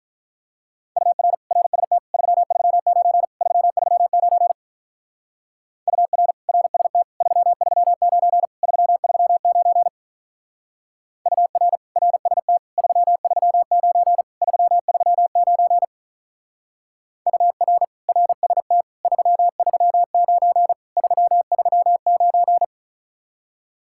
Morse code, heard as a keyed tone, about 42 wpm, 700 Hz. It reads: UR RST 339 339 UR RST 339 339 UR RST 339 339 UR RST 339 339
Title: Speed-Racing Common QSO Phrases - 35wpm